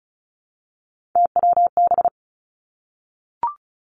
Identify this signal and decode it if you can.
Morse code, heard as a keyed tone, about 36 wpm, 700 Hz. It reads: TWB
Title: Three Challenging Letter-Number Combinations 35wpm